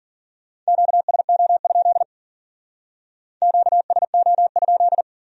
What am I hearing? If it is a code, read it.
Morse code, heard as a keyed tone, about 49 wpm, 700 Hz. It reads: QSO? QSO?
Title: Speed-Racing Q-Codes - 40wpm